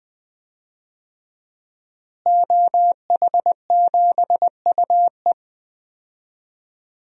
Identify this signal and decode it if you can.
Morse code, heard as a keyed tone, about 20 wpm, 700 Hz. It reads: OH7UE
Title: DXCC-100 Call Signs - 5 Characters - Repeated 3x - 20wpm